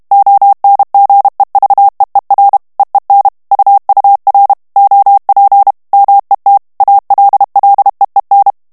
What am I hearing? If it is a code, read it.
Morse code, heard as a keyed tone, about 32 wpm, 800 Hz. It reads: ONGEVEER EEN UUR OP MET ALLEEN
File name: HOLLANDAIS texte clair 2 vitesse 30 mots minute